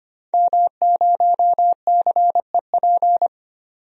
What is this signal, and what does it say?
Morse code, heard as a keyed tone, about 25 wpm, 700 Hz. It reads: M0CEP
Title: DXCC-100 Call Signs - 5 Characters - 25wpm